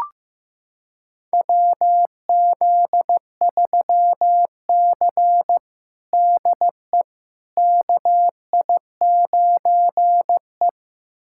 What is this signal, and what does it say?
Morse code, heard as a keyed tone, about 15 wpm, 700 Hz. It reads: WZ3C DE KI9E
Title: USA Callsign DE Callsign - 4 Characters - 15wpm